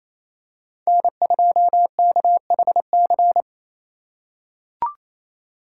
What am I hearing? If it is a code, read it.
Morse code, heard as a keyed tone, about 28 wpm, 700 Hz. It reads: N2KHC
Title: USA Call Signs - 5 Characters - 6x Character Spacing - 28wpm